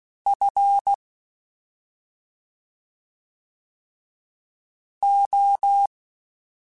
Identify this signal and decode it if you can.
Morse code, heard as a keyed tone, about 16 wpm, spaced as for 2 wpm, 800 Hz. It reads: FO
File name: Special INGs-mp3s / Special INGs-mp3s / Slow ing @10